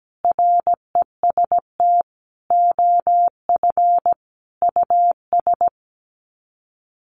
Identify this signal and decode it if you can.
Morse code, heard as a keyed tone, about 17 wpm, 700 Hz. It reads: REST OF US
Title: Sets of 5 Words Encore 17wpm